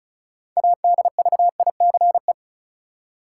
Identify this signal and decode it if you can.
Morse code, heard as a keyed tone, about 34 wpm, 700 Hz. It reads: ADVICE
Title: Six Letter Words - 35wpm